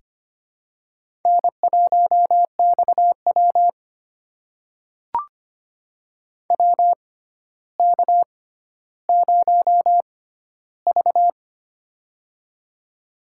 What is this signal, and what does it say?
Morse code, heard as a keyed tone, about 25 wpm, spaced as for 13 wpm, 700 Hz. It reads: N1XW WK0V
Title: USA Call Signs - 4 Characters - 6x Character Spacing - 25wpm